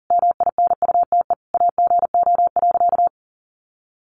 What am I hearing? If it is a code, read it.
Morse code, heard as a keyed tone, about 37 wpm, 700 Hz. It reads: MINUTE AGO.
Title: Sentences from Top 500 Words 40wpm